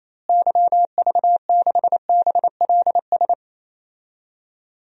Morse code, heard as a keyed tone, about 28 wpm, 700 Hz. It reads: YV6BLS
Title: DXCC-100 Call Signs - 6 Characters - Repeated 2x - 28wpm